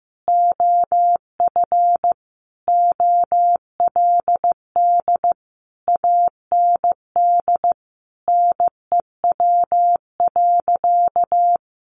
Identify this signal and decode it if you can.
Morse code, heard as a keyed tone, about 15 wpm, 700 Hz. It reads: OF OLD AND NEW.
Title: Sentences from Top 1500 Words 15wpm